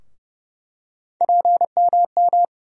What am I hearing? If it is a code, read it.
Morse code, heard as a keyed tone, about 30 wpm, 700 Hz. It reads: PMM
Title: USA Call Sign Postfixes - 3 Letters 30wpm